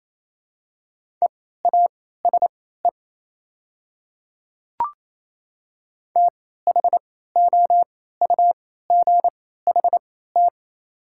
Morse code, heard as a keyed tone, about 28 wpm, spaced as for 16 wpm, 700 Hz. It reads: EASE THOUGHT
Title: 3x Character Spacing - Top 300 Words - 28wpm